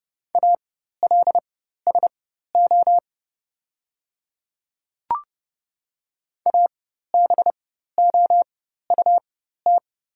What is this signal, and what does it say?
Morse code, heard as a keyed tone, about 30 wpm, spaced as for 14 wpm, 700 Hz. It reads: ALSO ABOUT
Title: 4x Character Spacing - Top 200 Words - 30wpm